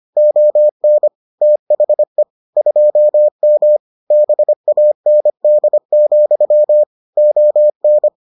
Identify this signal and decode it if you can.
Morse code, heard as a keyed tone, about 25 wpm, 600 Hz. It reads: ON THE 2M BAND, ON